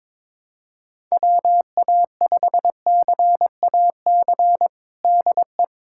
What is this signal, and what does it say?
Morse code, heard as a keyed tone, about 21 wpm, 700 Hz. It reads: WA5CAC DE
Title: USA Callsign DE Callsign 22wpm